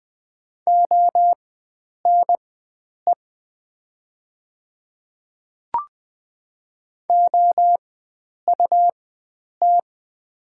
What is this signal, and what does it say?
Morse code, heard as a keyed tone, about 20 wpm, spaced as for 9 wpm, 700 Hz. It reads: ONE OUT
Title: 4x Character Spacing - Top 100 Words - 20wpm